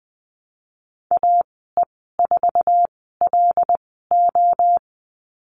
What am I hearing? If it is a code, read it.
Morse code, heard as a keyed tone, about 20 wpm, spaced as for 15 wpm, 700 Hz. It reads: AE4LO
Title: 082 Lesson 14g - Number 2 - Call Signs 20wpm